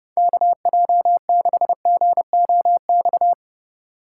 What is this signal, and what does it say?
Morse code, heard as a keyed tone, about 31 wpm, 700 Hz. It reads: KJ6GOX